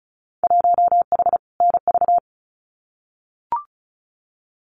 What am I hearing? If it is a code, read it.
Morse code, heard as a keyed tone, about 34 wpm, 700 Hz. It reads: 1H NV
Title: Winter Field Day Exchanges 35wpm